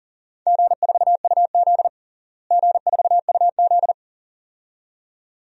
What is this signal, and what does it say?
Morse code, heard as a keyed tone, about 40 wpm, 700 Hz. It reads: G4UZ G4UZ
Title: DXCC-100 Call Signs - 4 Characters - Repeated 2x - 40wpm